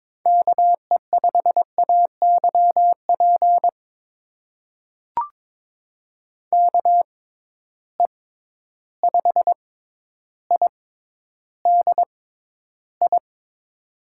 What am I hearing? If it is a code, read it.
Morse code, heard as a keyed tone, about 22 wpm, spaced as for 11 wpm, 700 Hz. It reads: KE5AYP KE5IDI
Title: USA Call Signs - 6 Characters - 6x Character Spacing - 22wpm